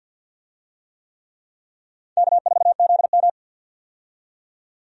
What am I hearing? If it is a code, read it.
Morse code, heard as a keyed tone, about 50 wpm, 700 Hz. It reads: K4ZM
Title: USA Call Signs - 4 Characters - 50wpm